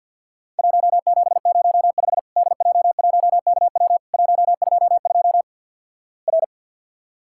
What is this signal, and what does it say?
Morse code, heard as a keyed tone, about 48 wpm, 700 Hz. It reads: 1705 DJ1KW 122 R
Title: Worked All Europe DX Contest - QTC Exchange with Repeats - 50wpm